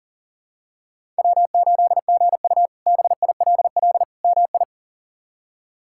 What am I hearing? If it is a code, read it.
Morse code, heard as a keyed tone, about 40 wpm, 700 Hz. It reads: W8GU BILL MI
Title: K1USN SST QSO Practice - 40wpm